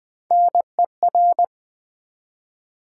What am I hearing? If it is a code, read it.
Morse code, heard as a keyed tone, about 20 wpm, 700 Hz. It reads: NER
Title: Build A Word - 1 Letter At A Time - 5-Letter Words - 20wpm